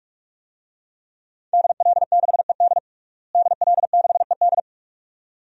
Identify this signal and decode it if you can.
Morse code, heard as a keyed tone, about 46 wpm, 700 Hz. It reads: DL6ED DL6ED
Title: DXCC-100 Call Signs - 5 Characters - Repeated 2x - 45wpm